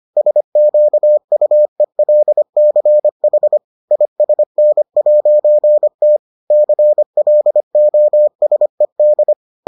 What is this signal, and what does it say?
Morse code, heard as a keyed tone, about 25 wpm, 600 Hz. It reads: SQUELCH ISN'T CLOSED